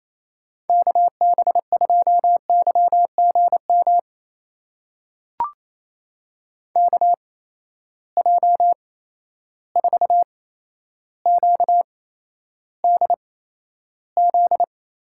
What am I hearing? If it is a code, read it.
Morse code, heard as a keyed tone, about 27 wpm, spaced as for 12 wpm, 700 Hz. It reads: KB2YGM KJ4QDZ
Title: USA Call Signs - 6 Characters - 8x Character Spacing - 28wpm